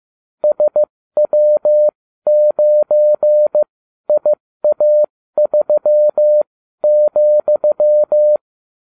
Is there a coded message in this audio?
Yes: Morse code from 0.4 to 8.4 s, heard as a keyed tone, about 15 wpm, spaced as for 12 wpm, 600 Hz. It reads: SW9IA3,